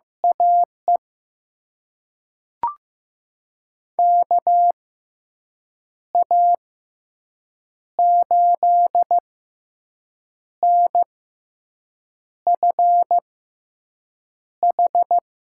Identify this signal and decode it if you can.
Morse code, heard as a keyed tone, about 15 wpm, spaced as for 6 wpm, 700 Hz. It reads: AE KA8NFH
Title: USA Call Signs - 6 Characters - 6x Character Spacing - 15wpm